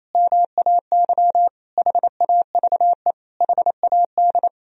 Morse code, heard as a keyed tone, about 28 wpm, 700 Hz. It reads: MAY HAVE HAD
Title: Sentences from Top 1000 Words - Review Part 2 - 28wpm